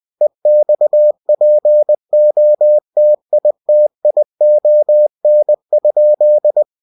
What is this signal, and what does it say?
Morse code, heard as a keyed tone, about 20 wpm, 600 Hz. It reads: EXPOTITION?